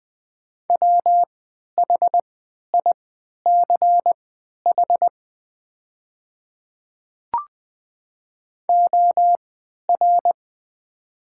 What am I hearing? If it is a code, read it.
Morse code, heard as a keyed tone, about 20 wpm, spaced as for 11 wpm, 700 Hz. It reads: WHICH OR
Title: 3x Character Spacing - Top 200 Words - 20wpm